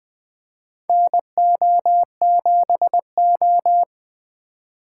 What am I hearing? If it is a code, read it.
Morse code, heard as a keyed tone, about 20 wpm, 700 Hz. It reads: NO7O
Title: USA Call Signs - 4 Characters - 3x Character Spacing - 20wpm